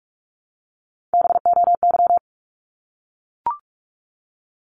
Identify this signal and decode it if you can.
Morse code, heard as a keyed tone, about 44 wpm, 700 Hz. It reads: BOY
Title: Top 200 Words 45wpm